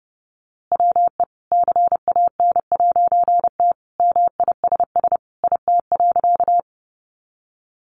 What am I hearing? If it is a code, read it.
Morse code, heard as a keyed tone, about 30 wpm, 700 Hz. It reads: WE CAN'T MISS IT.
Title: A Sound Of Thunder 30wpm